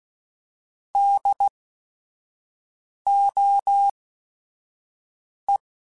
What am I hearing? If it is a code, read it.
Morse code, heard as a keyed tone, about 16 wpm, spaced as for 5 wpm, 800 Hz. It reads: DOE